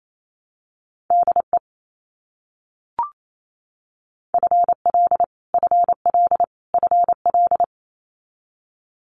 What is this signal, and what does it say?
Morse code, heard as a keyed tone, about 28 wpm, 700 Hz. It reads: DE FL FL FL
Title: US States Repeated 3x 28wpm